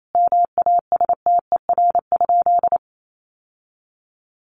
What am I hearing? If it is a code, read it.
Morse code, heard as a keyed tone, about 28 wpm, 700 Hz. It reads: MASTER?
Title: Sentences from Top 1300 Words 28wpm